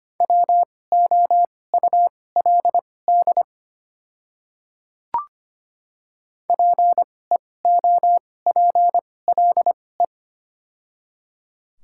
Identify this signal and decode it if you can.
Morse code, heard as a keyed tone, about 25 wpm, spaced as for 18 wpm, 700 Hz. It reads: WOULD PEOPLE
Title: 2x Character Spacing - Top 200 Words - 25wpm